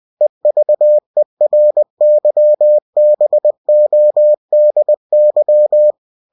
Morse code, heard as a keyed tone, about 20 wpm, 600 Hz. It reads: EVERYBODY
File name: cw-books / winnie-the-pooh / winnie-20wpm-page-0085